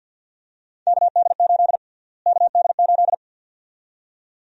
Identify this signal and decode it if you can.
Morse code, heard as a keyed tone, about 48 wpm, 700 Hz. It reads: KD8 KD8
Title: USA Call Sign Prefixes - Repeated 2x - 50wpm